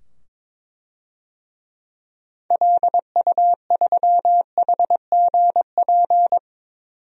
Morse code, heard as a keyed tone, about 22 wpm, 700 Hz. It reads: LU3HGP
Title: DXCC-100 Call Signs - 6 Characters - 22wpm